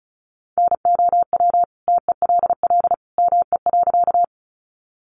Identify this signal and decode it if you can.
Morse code, heard as a keyed tone, about 34 wpm, 700 Hz. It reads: NOW TELL ME.
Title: Sentences from Top 200 Words 35wpm